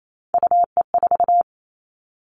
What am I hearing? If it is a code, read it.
Morse code, heard as a keyed tone, about 28 wpm, 700 Hz. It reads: UE4
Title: Three Challenging Letter-Number Combinations 28wpm